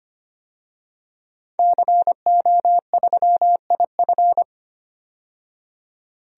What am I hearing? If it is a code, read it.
Morse code, heard as a keyed tone, about 25 wpm, 690 Hz. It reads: CO3IF
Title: DXCC-100 Call Signs - 5 Characters - 25wpm